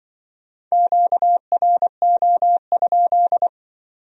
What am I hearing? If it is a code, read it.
Morse code, heard as a keyed tone, about 25 wpm, 700 Hz. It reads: QRO?
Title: Speed-Racing Q-Codes - 20wpm